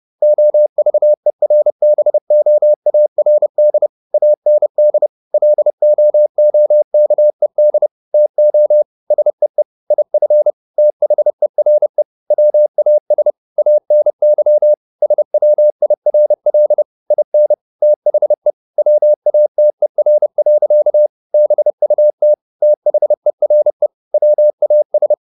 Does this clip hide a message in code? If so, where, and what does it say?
Yes: Morse code from 0.2 to 25.1 s, heard as a keyed tone, about 30 wpm, 600 Hz. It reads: OVERBOARD AND LOOKED TO SEE IF THERE WAS ANY SWIRL IN THE WATER. BUT THERE WAS